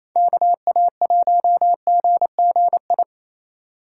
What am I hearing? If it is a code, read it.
Morse code, heard as a keyed tone, about 36 wpm, 700 Hz. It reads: KA1GGI